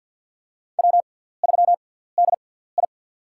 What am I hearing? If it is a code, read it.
Morse code, heard as a keyed tone, about 47 wpm, spaced as for 17 wpm, 700 Hz. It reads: W3DI
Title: USA Call Signs - 4 Characters - 6x Character Spacing - 50wpm